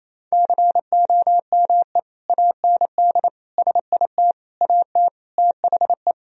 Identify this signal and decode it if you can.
Morse code, heard as a keyed tone, about 28 wpm, 700 Hz. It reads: COME AND SIT AT THE
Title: Sentences from Top 1000 Words - Review Part 2 - 28wpm